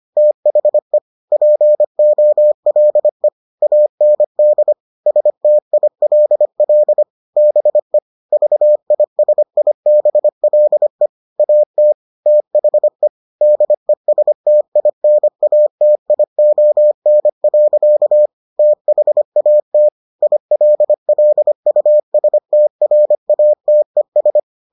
Morse code, heard as a keyed tone, about 25 wpm, 600 Hz. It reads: THE POLE AND STILL BE VISIBLE AT THE DESTINATION. THAT ILLUSTRATES